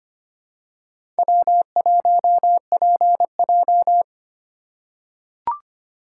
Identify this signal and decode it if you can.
Morse code, heard as a keyed tone, about 25 wpm, 700 Hz. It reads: W1PJ
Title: USA Call Signs - 4 Characters - 3x Character Spacing - 25wpm